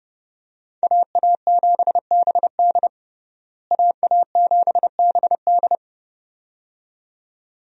Morse code, heard as a keyed tone, about 30 wpm, 700 Hz. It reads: AA7BD AA7BD
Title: USA Call Signs - 5 Characters - Repeated 2x - 30wpm